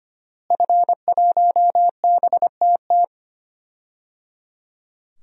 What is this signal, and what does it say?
Morse code, heard as a keyed tone, about 25 wpm, 700 Hz. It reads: F1BTT
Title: DXCC-100 Call Signs - 5 Characters - Repeated 3x - 25wpm